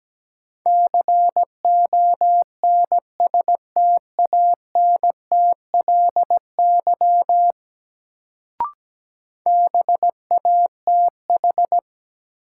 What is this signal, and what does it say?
Morse code, heard as a keyed tone, about 17 wpm, 700 Hz. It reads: CONSTANTLY BATH
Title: Sentences from Top 1600 Words - New Word Drill Alt - 17wpm